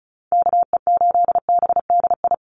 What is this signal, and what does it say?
Morse code, heard as a keyed tone, about 34 wpm, 700 Hz. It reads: KE8BDI
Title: USA Call Signs 35wpm